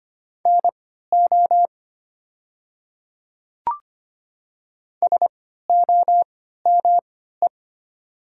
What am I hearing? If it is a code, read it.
Morse code, heard as a keyed tone, about 25 wpm, spaced as for 14 wpm, 700 Hz. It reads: NO SOME
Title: 3x Character Spacing - Top 100 Words - 25wpm